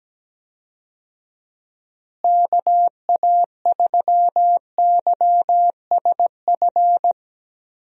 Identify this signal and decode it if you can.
Morse code, heard as a keyed tone, about 17 wpm, 700 Hz. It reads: KA3YSF
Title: USA Call Signs - 6 Characters - Repeated 3x - 17wpm